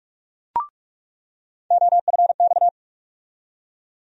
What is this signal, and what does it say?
Morse code, heard as a keyed tone, about 42 wpm, 700 Hz. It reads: OFX
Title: USA Call Sign Postfixes - 3 Letters 45wpm